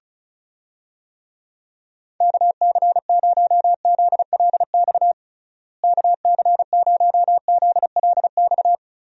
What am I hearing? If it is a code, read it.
Morse code, heard as a keyed tone, about 34 wpm, 700 Hz. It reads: KC0ZLX KC0ZLX
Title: USA Call Signs - 6 Characters - Repeated 2x - 35wpm